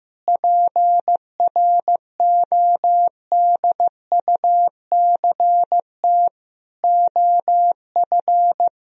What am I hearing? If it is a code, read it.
Morse code, heard as a keyed tone, about 15 wpm, 700 Hz. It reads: PRODUCT OF
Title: Sentences from Top 1000 Words - Review Part 2 - 15wpm